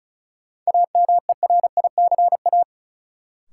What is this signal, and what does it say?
Morse code, heard as a keyed tone, about 36 wpm, 700 Hz. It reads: AMERICA
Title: Sets of 5 Words - New Word Drill - 35wpm